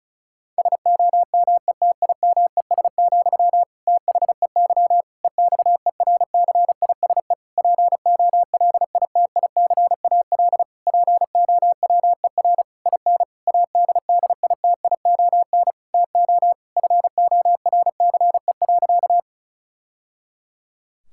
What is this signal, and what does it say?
Morse code, heard as a keyed tone, about 36 wpm, 700 Hz. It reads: SOMETIMES, THEY EXERCISE POLITICAL POWER IN ADDITION TO FORCE.
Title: Sentences from Top 1000 Words - Review Part 6 - 35wpm